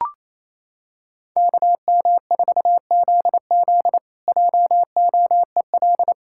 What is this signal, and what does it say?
Morse code, heard as a keyed tone, about 28 wpm, 700 Hz. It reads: KM4ZZ JOEL
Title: K1USN SST QSO Practice - 28wpm